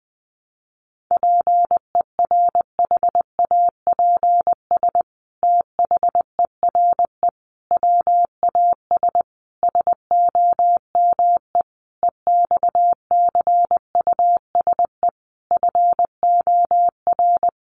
Morse code, heard as a keyed tone, about 19 wpm, 700 Hz. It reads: PERHAPS THERE WAS SOME EXCUSE FOR
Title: Sentences from Top 700 Words 20wpm